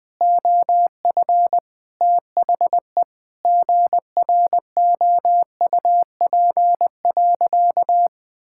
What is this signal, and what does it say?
Morse code, heard as a keyed tone, about 20 wpm, 700 Hz. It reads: OF THE GROUP.
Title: Sentences with 5-Word Phrases - 20wpm